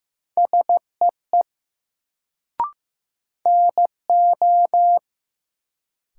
Morse code, heard as a keyed tone, about 16 wpm, 700 Hz. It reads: SEE NO
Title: Sentences from Top 100 Words - New Word Drill - 15wpm